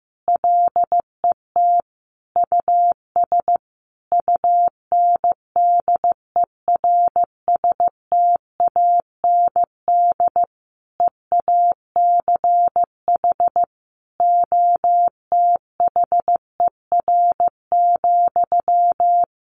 Morse code, heard as a keyed tone, about 15 wpm, 700 Hz. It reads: LET US UNDERSTAND EACH OTHER,
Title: Sentences from Top 1500 Words 15wpm